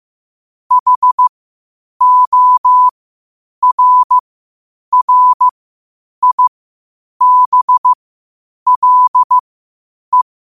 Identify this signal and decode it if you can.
Morse code, heard as a keyed tone, about 15 wpm, spaced as for 8 wpm, 1000 Hz. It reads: HORRIBLE
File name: qotdfiles / 8wpm